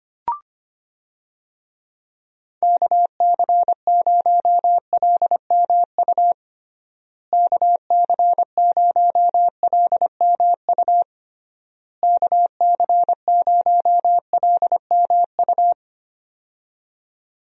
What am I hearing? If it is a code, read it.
Morse code, heard as a keyed tone, about 25 wpm, 700 Hz. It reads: KC0LMU KC0LMU KC0LMU
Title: USA Call Signs - 6 Characters - Repeated 3x - 25wpm